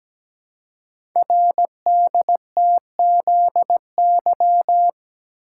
Morse code, heard as a keyed tone, about 17 wpm, 700 Hz. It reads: RDTZY
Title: Code Groups - 5 Letters - 17wpm